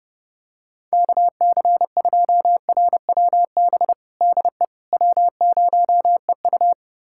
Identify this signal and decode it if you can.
Morse code, heard as a keyed tone, about 30 wpm, 700 Hz. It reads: KC2RWB DE W0EU
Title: USA Callsign DE Callsign 30wpm